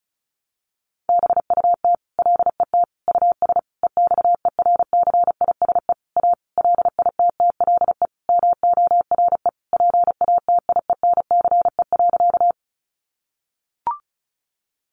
Morse code, heard as a keyed tone, about 34 wpm, 700 Hz. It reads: BUT LET US EXERCISE A LITTLE MORE PATIENCE.
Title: Sentences from Top 1300 Words 35wpm